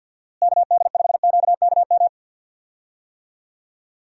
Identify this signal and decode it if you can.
Morse code, heard as a keyed tone, about 47 wpm, 700 Hz. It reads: KD5QKM
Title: USA Call Signs - 6 Characters - Repeated 3x - 50wpm